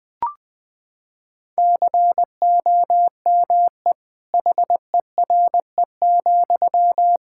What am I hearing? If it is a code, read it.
Morse code, heard as a keyed tone, about 21 wpm, 700 Hz. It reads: COME HERE,
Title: Sentences from Top 1600 Words 20wpm